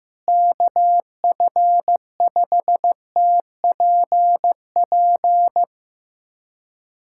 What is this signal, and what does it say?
Morse code, heard as a keyed tone, about 15 wpm, 700 Hz. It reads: KF5TPP
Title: USA Callsign DE Callsign - 6 Characters - 15wpm